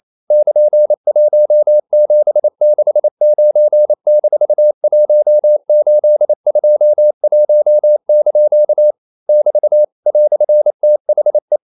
Morse code, heard as a keyed tone, about 28 wpm, 600 Hz. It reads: <KN>1769-1821) <BT> "THE
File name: CW / Practice / EnglishQuotations0128